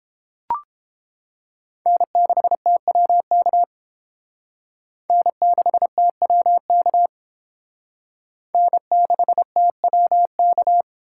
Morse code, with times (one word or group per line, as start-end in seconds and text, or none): 1.9-3.6 N6TWK
5.1-7.1 N6TWK
8.5-10.8 N6TWK